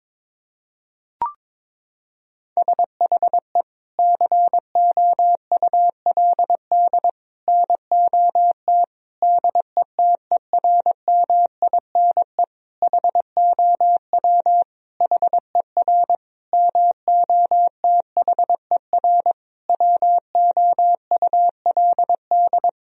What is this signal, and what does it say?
Morse code, heard as a keyed tone, about 22 wpm, 700 Hz. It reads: SHE COULD NOT DETERMINE HOW HER MOTHER WOULD
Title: Sentences from Top 1000 Words - Review Part 4 - 22wpm